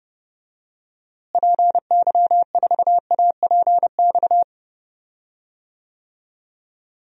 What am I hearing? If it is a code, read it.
Morse code, heard as a keyed tone, about 30 wpm, 700 Hz. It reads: PY4APX
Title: DXCC-100 Call Signs - 6 Characters - Repeated 3x - 30wpm